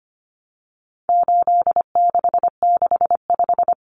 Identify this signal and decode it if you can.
Morse code, heard as a keyed tone, about 25 wpm, 700 Hz. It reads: 8665
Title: Common Cut Numbers - 4-Digit 25wpm